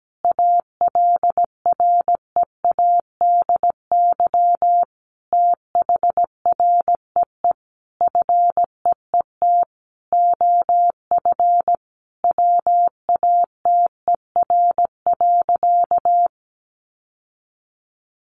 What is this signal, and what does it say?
Morse code, heard as a keyed tone, about 17 wpm, 700 Hz. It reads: ALREADY THREE FEET OF WATER.
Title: Sentences from Top 700 Words 17wpm